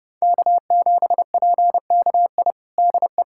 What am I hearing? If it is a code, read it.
Morse code, heard as a keyed tone, about 30 wpm, 700 Hz. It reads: K7PKI DE
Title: USA Callsign DE Callsign - 5 Characters - 30wpm